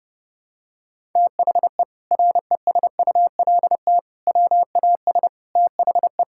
Morse code, heard as a keyed tone, about 28 wpm, 700 Hz. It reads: THE RESULT WAS THE
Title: Sentences from Top 1000 Words - Review Part 5 - 30wpm